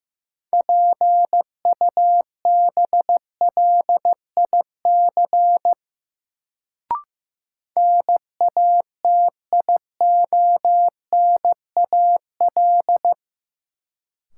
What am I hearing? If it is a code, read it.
Morse code, heard as a keyed tone, about 15 wpm, 700 Hz. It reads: PUBLIC NATIONAL
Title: Sets of 2 Words - New Word Drill - 15wpm